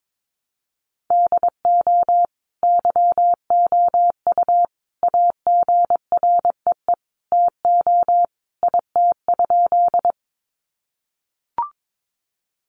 Morse code, heard as a keyed tone, about 22 wpm, 700 Hz. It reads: DO YOU AGREE TO IT?
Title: Sentences from Top 1100 Words 22wpm